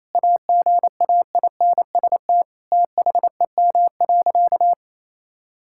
Morse code, heard as a keyed tone, about 28 wpm, 700 Hz. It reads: AGAINST THEM.
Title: Sentences from Top 1000 Words - Review Part 6 - 28wpm